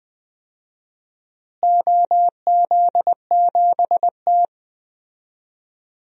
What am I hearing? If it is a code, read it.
Morse code, heard as a keyed tone, about 20 wpm, 700 Hz. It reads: OZ7T
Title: DXCC-100 Call Signs - 4 Characters - Repeated 3x - 20wpm